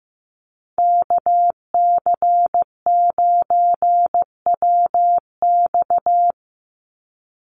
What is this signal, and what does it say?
Morse code, heard as a keyed tone, about 15 wpm, 700 Hz. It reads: KC9WX
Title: USA Call Signs 15wpm